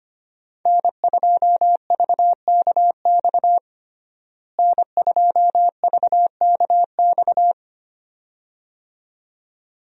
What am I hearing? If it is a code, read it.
Morse code, heard as a keyed tone, about 25 wpm, 700 Hz. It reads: N2VKX N2VKX
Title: USA Call Signs - 5 Characters - Repeated 2x - 25wpm